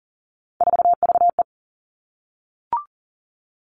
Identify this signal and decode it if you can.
Morse code, heard as a keyed tone, about 42 wpm, 690 Hz. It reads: 4VE
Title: Three Challenging Letter-Number Combinations 40wpm